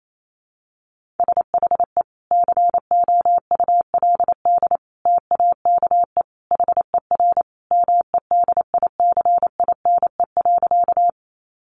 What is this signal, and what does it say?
Morse code, heard as a keyed tone, about 28 wpm, 690 Hz. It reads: SHE COULD TAKE HER MEDICINE.
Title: Sentences from Top 900 Words 28wpm